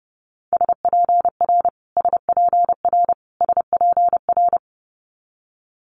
Morse code, heard as a keyed tone, about 30 wpm, 700 Hz. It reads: SPR SPR SPR
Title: Multi-letter Phonemes Repeated 3x 30wpm